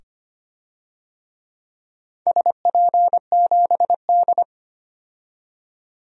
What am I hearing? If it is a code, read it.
Morse code, heard as a keyed tone, about 25 wpm, 700 Hz. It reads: SP7D
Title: DXCC-100 Call Signs - 4 Characters - 25wpm